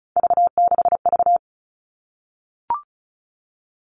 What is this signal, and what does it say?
Morse code, heard as a keyed tone, about 34 wpm, 700 Hz. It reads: V6V
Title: All Cut Numbers - 3-Digit 35wpm